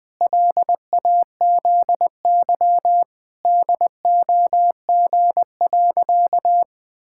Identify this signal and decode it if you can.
Morse code, heard as a keyed tone, about 20 wpm, 700 Hz. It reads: LAZY DOG.
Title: Pangram Challenge v2 - 20wpm